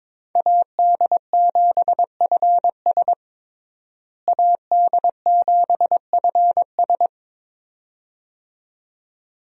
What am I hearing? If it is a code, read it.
Morse code, heard as a keyed tone, about 22 wpm, 700 Hz. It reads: AD7FS AD7FS
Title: USA Call Signs - 5 Characters - Repeated 2x - 22wpm